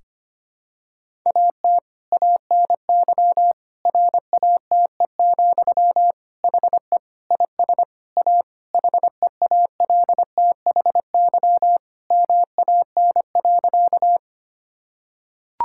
Morse code, heard as a keyed tone, about 25 wpm, 700 Hz. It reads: AT ANY RATE, HE IS A HEALTHY MAN.
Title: Sentences from Top 1000 Words - Review Part 5 - 25wpm